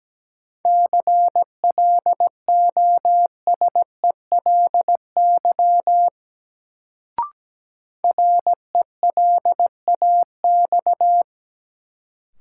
Morse code, heard as a keyed tone, about 17 wpm, 700 Hz. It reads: CLOSELY RELAX
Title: Sentences from Top 1400 Words - New Word Drill - 17wpm